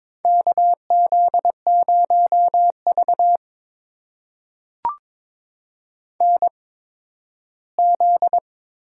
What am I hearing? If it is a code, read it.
Morse code, heard as a keyed tone, about 22 wpm, spaced as for 13 wpm, 700 Hz. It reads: KZ0V NZ
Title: USA Call Signs - 4 Characters - 8x Character Spacing - 22wpm